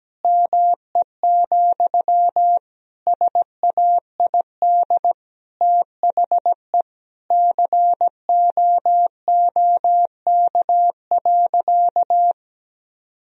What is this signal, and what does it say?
Morse code, heard as a keyed tone, about 17 wpm, 700 Hz. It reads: ME, SAID THE COOK.
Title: Sentences from Top 1000 Words - Review Part 4 - 17wpm